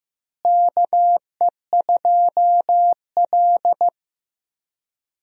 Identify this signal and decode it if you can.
Morse code, heard as a keyed tone, about 15 wpm, 700 Hz. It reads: KE2L
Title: USA Call Signs - 4 Characters - Repeated 3x - 15wpm